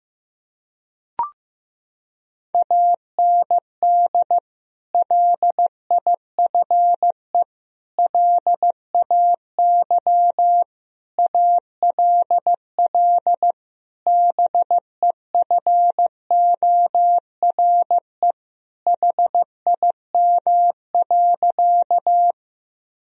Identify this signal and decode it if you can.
Morse code, heard as a keyed tone, about 15 wpm, 700 Hz. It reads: AND LIFE LAY ALL BEFORE HIM.
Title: Sentences from Top 1000 Words - Review Part 2 - 15wpm